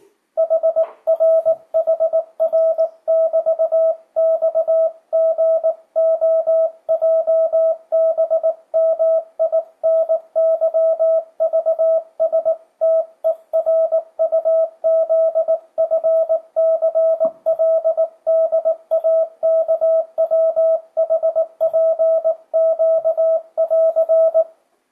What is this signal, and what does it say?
Morse code, heard as a keyed tone, about 18 wpm, 650 Hz. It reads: HRHR<BT>XGOJBMINYVSTERUZFCLDAKWHPQ<AR>
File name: 2025cw / speedup / 19wpm